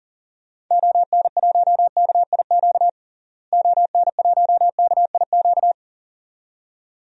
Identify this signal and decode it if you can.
Morse code, heard as a keyed tone, about 40 wpm, 700 Hz. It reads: ON1KIQ ON1KIQ
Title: DXCC-100 Call Signs - 6 Characters - Repeated 2x - 40wpm